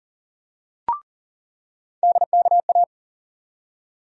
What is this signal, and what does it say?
Morse code, heard as a keyed tone, about 40 wpm, 700 Hz. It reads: DKA
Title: USA Call Sign Postfixes - 3 Letters 40wpm